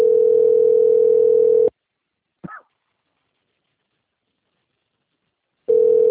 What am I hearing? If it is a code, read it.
Ringback tone (North America), 440 + 480 Hz.
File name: out- - - - - .4675